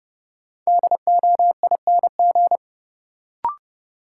Morse code, heard as a keyed tone, about 30 wpm, 700 Hz. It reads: DOING
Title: Sentences from Top 1400 Words - New Word Drill Alt - 30wpm